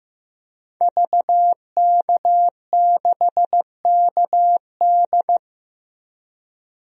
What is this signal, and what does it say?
Morse code, heard as a keyed tone, about 15 wpm, 700 Hz. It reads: VK6KD
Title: DXCC-100 Call Signs - 5 Characters - 15wpm